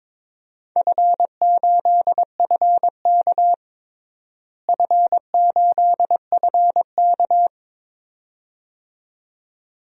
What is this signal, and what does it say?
Morse code, heard as a keyed tone, about 22 wpm, 700 Hz. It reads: F8FK F8FK
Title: DXCC-100 Call Signs - 4 Characters - Repeated 2x - 22wpm